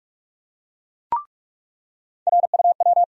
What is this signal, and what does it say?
Morse code, heard as a keyed tone, about 45 wpm, 700 Hz. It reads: RUW
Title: USA Call Sign Postfixes - 3 Letters 45wpm